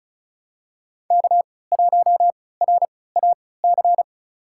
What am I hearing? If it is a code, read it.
Morse code, heard as a keyed tone, about 36 wpm, spaced as for 20 wpm, 700 Hz. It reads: K1RAC
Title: USA Call Signs - 5 Characters - 3x Character Spacing - 35wpm